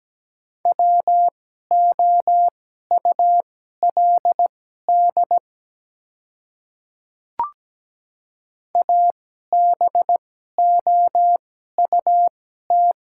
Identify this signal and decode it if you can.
Morse code, heard as a keyed tone, about 17 wpm, spaced as for 12 wpm, 700 Hz. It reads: WOULD ABOUT
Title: 2x Character Spacing - Top 100 Words - 17wpm